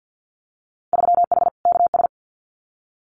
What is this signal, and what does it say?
Morse code, heard as a keyed tone, about 50 wpm, 700 Hz. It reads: 3H KS
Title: Winter Field Day Exchanges Encore 50wpm